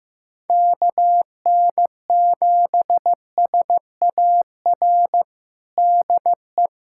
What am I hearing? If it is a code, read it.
Morse code, heard as a keyed tone, about 16 wpm, 700 Hz. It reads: KN7SAR DE
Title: USA Callsign DE Callsign 15wpm